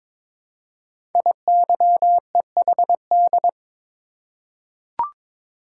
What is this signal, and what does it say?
Morse code, heard as a keyed tone, about 22 wpm, 700 Hz. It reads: IYEHD
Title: Code Groups - 5 Letters - 22wpm